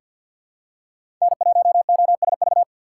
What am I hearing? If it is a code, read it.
Morse code, heard as a keyed tone, about 48 wpm, 700 Hz. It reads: N1OSV
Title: USA Call Signs - 5 Characters - 6x Character Spacing - 50wpm